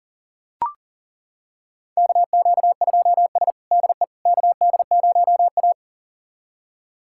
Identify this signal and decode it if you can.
Morse code, heard as a keyed tone, about 40 wpm, 700 Hz. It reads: KQ2S DE KD0A
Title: USA Callsign DE Callsign - 4 Characters - 40wpm